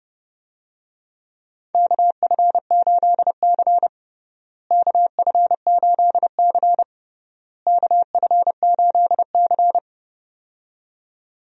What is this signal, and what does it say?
Morse code, heard as a keyed tone, about 32 wpm, 700 Hz. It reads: KF8C KF8C KF8C
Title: USA Call Signs - 4 Characters - Repeated 3x - 30wpm